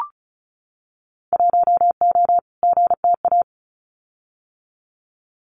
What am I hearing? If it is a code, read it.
Morse code, heard as a keyed tone, about 34 wpm, 700 Hz. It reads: 1O GTA
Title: Winter Field Day Exchanges 35wpm